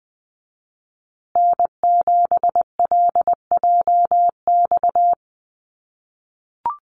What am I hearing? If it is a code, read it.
Morse code, heard as a keyed tone, about 20 wpm, 700 Hz. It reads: N7LJX
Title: USA Call Signs Repeated 2x 20wpm